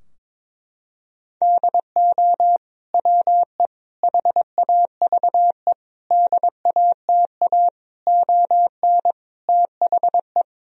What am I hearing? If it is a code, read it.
Morse code, heard as a keyed tone, about 22 wpm, 700 Hz. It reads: DO WE HAVE DATA ON THE
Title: Sentences from Top 1000 Words - Review Part 3 - 22wpm